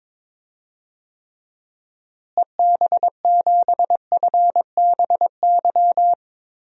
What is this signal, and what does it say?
Morse code, heard as a keyed tone, about 21 wpm, 700 Hz. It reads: EB7FBY
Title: DXCC-100 Call Signs - 6 Characters - Repeated 3x - 22wpm